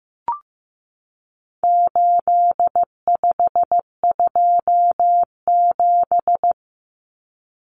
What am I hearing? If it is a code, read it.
Morse code, heard as a keyed tone, about 15 wpm, 700 Hz. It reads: 8527
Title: Common Cut Numbers - 4-Digit 15wpm